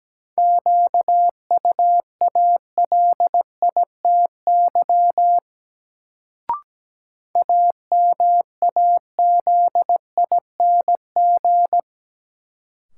Morse code, heard as a keyed tone, about 17 wpm, 700 Hz. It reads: QUALITY AMAZING
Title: Sentences from Top 1500 Words - New Word Drill Alt - 17wpm